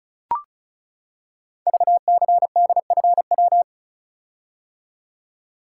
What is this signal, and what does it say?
Morse code, heard as a keyed tone, about 36 wpm, 700 Hz. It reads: VCDFW
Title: Code Groups - 5 Letters - 35wpm